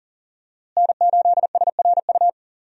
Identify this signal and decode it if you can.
Morse code, heard as a keyed tone, about 40 wpm, 700 Hz. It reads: N8SRU
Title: USA Call Signs - 5 Characters - Repeated 2x - 40wpm